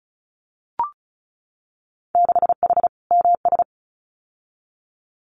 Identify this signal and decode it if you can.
Morse code, heard as a keyed tone, about 36 wpm, 700 Hz. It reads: 6H MS
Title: Winter Field Day Exchanges Encore 35wpm